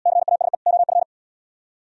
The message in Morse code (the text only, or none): CASE OF